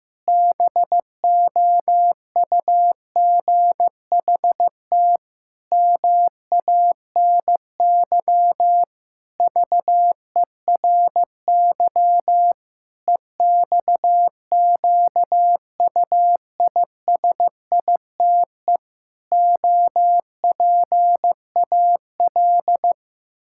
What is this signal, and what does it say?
Morse code, heard as a keyed tone, about 14 wpm, 700 Hz. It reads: BOUGHT MANY VERY EXQUISITE OPAL
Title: Pangram Challenge v2 - 15wpm